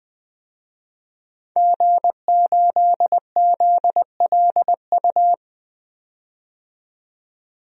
Morse code, heard as a keyed tone, about 20 wpm, 700 Hz. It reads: G8ZLU
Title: DXCC-100 Call Signs - 5 Characters - 20wpm